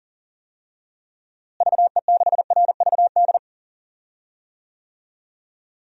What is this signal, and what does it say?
Morse code, heard as a keyed tone, about 40 wpm, 700 Hz. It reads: VE6RVD
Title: DXCC-100 Call Signs - 6 Characters - Repeated 3x - 40wpm